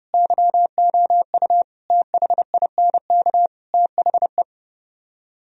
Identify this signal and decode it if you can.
Morse code, heard as a keyed tone, about 40 wpm, 700 Hz. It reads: YOU THINK THE